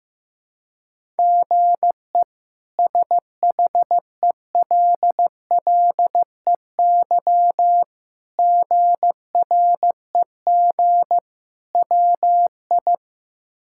Morse code, heard as a keyed tone, about 15 wpm, 700 Hz. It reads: GE SHELLEY GREG WI
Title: K1USN SST QSO Practice - 15wpm